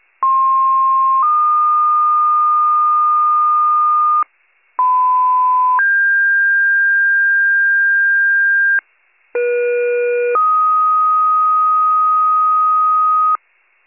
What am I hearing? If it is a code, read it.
Two-tone page, heard as a one-second tone, then a three-second tone, about 1060 then 1220 Hz, about 1010 then 1600 Hz, about 480 then 1250 Hz. It is heard from 0.2 to 13.4 s.